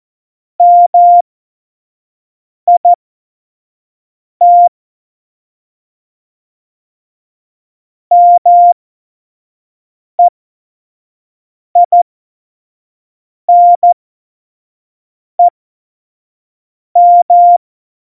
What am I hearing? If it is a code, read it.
Morse code, heard as a keyed tone, about 14 wpm, spaced as for 5 wpm, 700 Hz. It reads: MIT MEINEM
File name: Morsekurs / Witze / Witz 30. Januar 2018, 25 F70